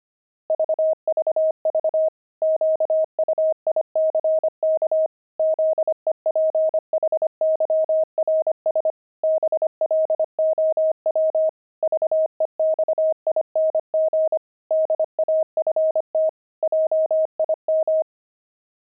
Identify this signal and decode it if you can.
Morse code, heard as a keyed tone, about 25 wpm, 630 Hz. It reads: VVV QUICK ZEPHYRS BLOW VEXING DAFT JIM
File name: pangram-fs-25-cs-25-zebras